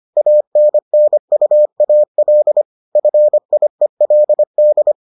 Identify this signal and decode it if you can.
Morse code, heard as a keyed tone, about 25 wpm, 600 Hz. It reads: ANNUAL FIELD